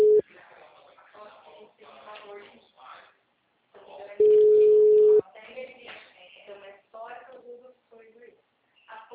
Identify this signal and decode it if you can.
Ringback tone (Europe, 425 Hz), with speech.